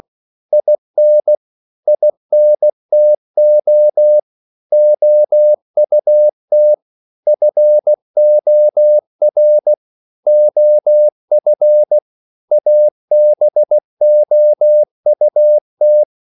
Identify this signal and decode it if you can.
Morse code, heard as a keyed tone, about 16 wpm, 600 Hz. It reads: IN INTO OUT FOR OF ABOUT
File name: CW / Practice / 500MostCommonEnglishWords16